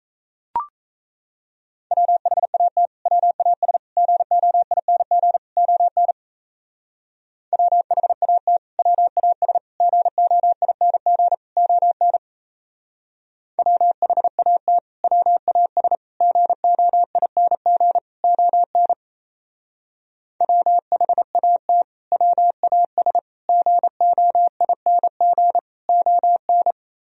Morse code, heard as a keyed tone, about 36 wpm, 700 Hz. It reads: WHAT WAS GOING ON WHAT WAS GOING ON WHAT WAS GOING ON WHAT WAS GOING ON